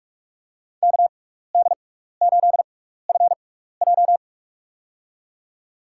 Morse code, heard as a keyed tone, about 45 wpm, spaced as for 15 wpm, 700 Hz. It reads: KD8FJ
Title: USA Call Signs - 5 Characters - 6x Character Spacing - 45wpm